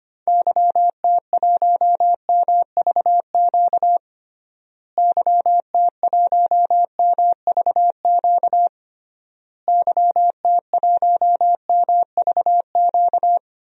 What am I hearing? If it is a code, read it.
Morse code, heard as a keyed tone, about 25 wpm, 700 Hz. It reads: YT1MVQ YT1MVQ YT1MVQ
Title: DXCC-100 Call Signs - 6 Characters - Repeated 3x - 25wpm